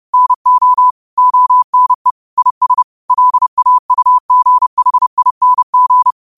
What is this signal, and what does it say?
Morse code, heard as a keyed tone, about 30 wpm, 1000 Hz. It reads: NO ONE IS LAUGHING
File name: qotdfiles / 30wpm